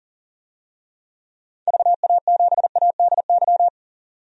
Morse code, heard as a keyed tone, about 40 wpm, 700 Hz. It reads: VA7ADY
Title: DXCC-100 Call Signs - 6 Characters - 40wpm